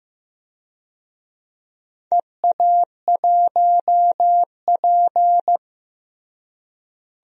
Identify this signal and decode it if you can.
Morse code, heard as a keyed tone, about 15 wpm, 700 Hz. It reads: EA1P
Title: DXCC-100 Call Signs - 4 Characters - Repeated 3x - 15wpm